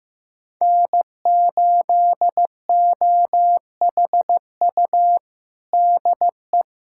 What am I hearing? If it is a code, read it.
Morse code, heard as a keyed tone, about 15 wpm, 700 Hz. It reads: N8OHU DE
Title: USA Callsign DE Callsign - 5 Characters - 15wpm